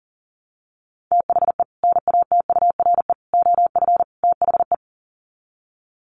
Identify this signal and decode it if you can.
Morse code, heard as a keyed tone, about 40 wpm, 700 Hz. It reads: THE NATURE OF THE
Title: Sets of 4 Words 40wpm